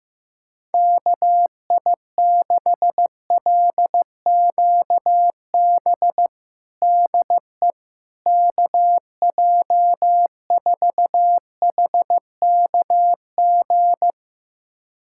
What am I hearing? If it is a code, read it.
Morse code, heard as a keyed tone, about 15 wpm, 700 Hz. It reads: KI6LQB DE KJ4HKG
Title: USA Callsign DE Callsign - 6 Characters - 15wpm